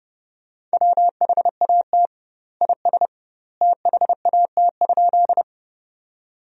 Morse code, heard as a keyed tone, about 30 wpm, 700 Hz. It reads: WHAT IS THAT?
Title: 2x Word-Spacing - Sentences from Top 100 Words - 30wpm